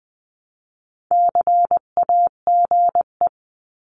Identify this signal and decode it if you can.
Morse code, heard as a keyed tone, about 20 wpm, 700 Hz. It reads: CAGE
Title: 259 Lesson 42d - Words 18-20wpm